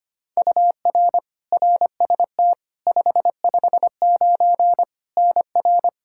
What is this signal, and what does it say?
Morse code, heard as a keyed tone, about 25 wpm, 700 Hz. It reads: UR RST 559 NR
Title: K1USN SST QSO Practice 25wpm